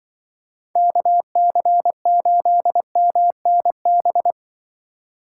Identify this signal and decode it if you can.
Morse code, heard as a keyed tone, about 24 wpm, 700 Hz. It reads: KC8MNB
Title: Speed Racing - USA 2x3 Call Signs 20wpm